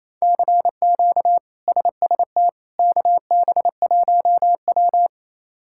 Morse code, heard as a keyed tone, about 27 wpm, 700 Hz. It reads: CQ SST KB1W
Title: K1USN SST QSO Practice - 28wpm